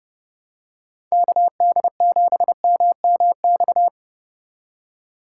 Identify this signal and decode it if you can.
Morse code, heard as a keyed tone, about 30 wpm, 700 Hz. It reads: KD7MMX
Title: USA Call Signs - 6 Characters - 4x Character Spacing - 30wpm